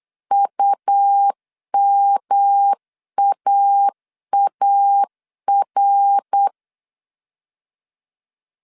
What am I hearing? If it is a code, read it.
Morse code, heard as a keyed tone, about 8 wpm, 800 Hz. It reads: UMAAR
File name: CW / DET / 1 3 1500C S C 8wpm 4214